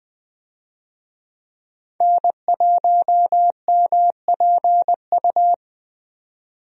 Morse code, heard as a keyed tone, about 20 wpm, 700 Hz. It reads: N1MPU
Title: USA Call Signs - 5 Characters - 20wpm